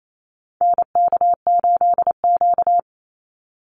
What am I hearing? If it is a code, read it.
Morse code, heard as a keyed tone, about 28 wpm, 700 Hz. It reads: NK8Q
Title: NA SOTA Activator Callsigns 28wpm